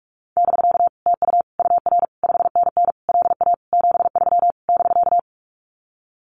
Morse code, heard as a keyed tone, about 44 wpm, 700 Hz. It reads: <BK> TU UR 5NN LA 73 <BK>
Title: Common QSO Phrases Encore 45wpm